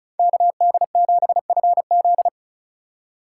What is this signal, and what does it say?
Morse code, heard as a keyed tone, about 36 wpm, 700 Hz. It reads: KD7FZ
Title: USA Call Signs - 5 Characters - 2x Character Spacing - 35wpm